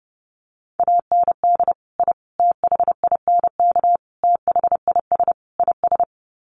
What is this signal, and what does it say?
Morse code, heard as a keyed tone, about 28 wpm, 700 Hz. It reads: AND I THINK THIS IS
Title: Sets of 5 Words Encore 30wpm